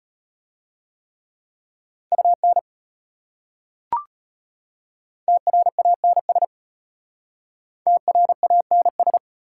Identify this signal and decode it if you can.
Morse code, heard as a keyed tone, about 32 wpm, 690 Hz. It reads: UN TRANS TRANS